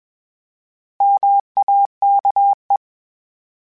Morse code, heard as a keyed tone, about 21 wpm, 800 Hz. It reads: MAKE
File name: Misc / Misc-mp3s / Misc @20